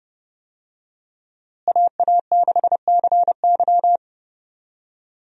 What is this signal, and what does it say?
Morse code, heard as a keyed tone, about 32 wpm, 700 Hz. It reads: AA6CY
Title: USA Call Signs - 5 Characters - Repeated 3x - 30wpm